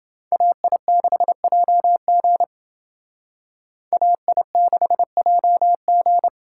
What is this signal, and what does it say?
Morse code, heard as a keyed tone, about 28 wpm, 700 Hz. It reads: AI6JG AI6JG